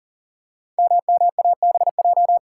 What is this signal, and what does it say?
Morse code, heard as a keyed tone, about 43 wpm, 700 Hz. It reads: MMABJ
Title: Code Groups - 5 Letter:Numbers - 40wpm